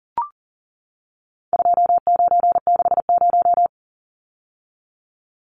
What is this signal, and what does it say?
Morse code, heard as a keyed tone, about 40 wpm, 700 Hz. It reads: 2960
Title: Numbers - 4-Digit 40wpm